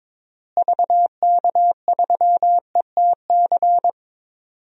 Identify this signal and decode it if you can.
Morse code, heard as a keyed tone, about 22 wpm, 700 Hz. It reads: VK3ETC
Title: DXCC-100 Call Signs - 6 Characters - Repeated 3x - 22wpm